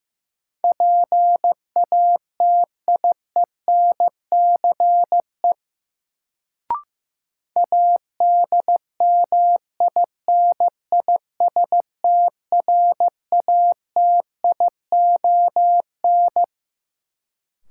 Morse code, heard as a keyed tone, about 15 wpm, 700 Hz. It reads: PATIENCE ADMINISTRATION
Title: Sentences from Top 1300 Words - New Word Drill Alt - 15wpm